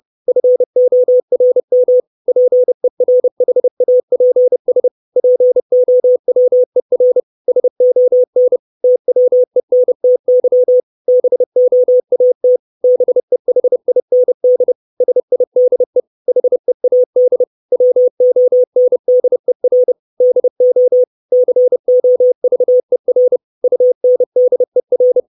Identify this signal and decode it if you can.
Morse code, heard as a keyed tone, about 30 wpm, 500 Hz. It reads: FORM PERHAPS POWER SON TWENTY BOAT BEHIND SIDE HEAD WONDER DO COVER UNDER
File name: 632 Words 30WPM